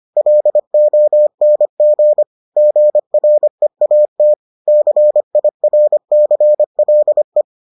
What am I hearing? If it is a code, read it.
Morse code, heard as a keyed tone, about 25 wpm, 600 Hz. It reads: LONG GREAT CIRCLE